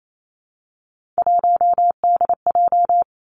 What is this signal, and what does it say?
Morse code, heard as a keyed tone, about 28 wpm, 700 Hz. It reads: 1DJ
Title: Three Challenging Letter-Number Combinations 28wpm